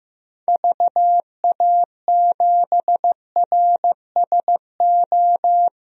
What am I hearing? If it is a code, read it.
Morse code, heard as a keyed tone, about 15 wpm, 700 Hz. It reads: VA7RSO
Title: DXCC-100 Call Signs - 6 Characters - 15wpm